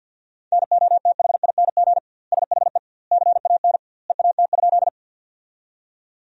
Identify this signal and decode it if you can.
Morse code, heard as a keyed tone, about 50 wpm, 700 Hz. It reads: NOTHING SHE CAN EAT?
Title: 2x Word-Spacing - Sentences from Top 500 Words - 50wpm